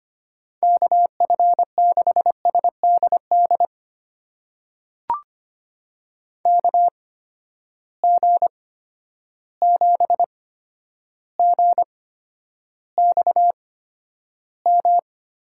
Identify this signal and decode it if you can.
Morse code, heard as a keyed tone, about 25 wpm, spaced as for 11 wpm, 700 Hz. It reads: KF6SDD KG7GXM
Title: USA Call Signs - 6 Characters - 8x Character Spacing - 25wpm